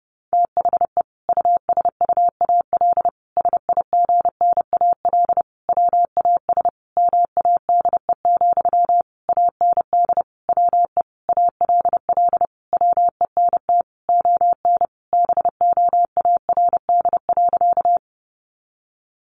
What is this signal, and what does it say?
Morse code, heard as a keyed tone, about 30 wpm, 700 Hz. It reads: THE USUAL SIGNAL WAS MADE, AND WE ALL WENT ON BOARD.
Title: Sentences from Top 1400 Words 30wpm